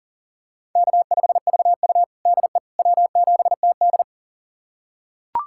Morse code, heard as a keyed tone, about 40 wpm, 700 Hz. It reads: K5VU DE W7TD
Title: USA Callsign DE Callsign - 4 Characters - 40wpm